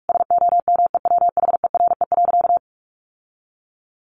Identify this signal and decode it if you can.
Morse code, heard as a keyed tone, about 44 wpm, 700 Hz. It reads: SOMEWHERE.
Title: Sentences from Top 1400 Words 45wpm